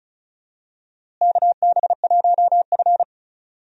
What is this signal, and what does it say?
Morse code, heard as a keyed tone, about 36 wpm, 700 Hz. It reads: KB1F
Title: USA Call Signs - 4 Characters - Repeated 3x - 35wpm